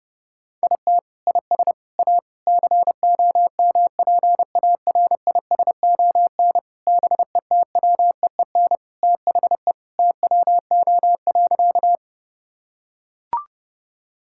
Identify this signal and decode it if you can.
Morse code, heard as a keyed tone, about 30 wpm, 700 Hz. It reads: IT IS A COMPARISON BETWEEN THE TWO.
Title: Sentences from Top 1600 Words 30wpm